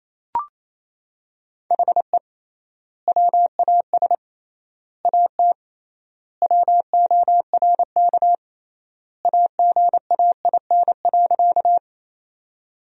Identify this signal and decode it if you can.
Morse code, heard as a keyed tone, about 28 wpm, 700 Hz. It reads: HE WAS AT WORK AGAIN.
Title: 3x Word-Spacing - Sentences from Top 200 Words - 28wpm